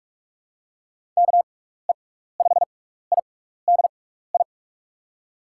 Morse code, heard as a keyed tone, about 45 wpm, spaced as for 15 wpm, 700 Hz. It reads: KE5IDI
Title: USA Call Signs - 6 Characters - 6x Character Spacing - 45wpm